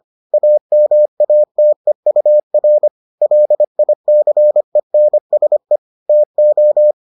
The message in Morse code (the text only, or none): AMATEUR LICENSE TO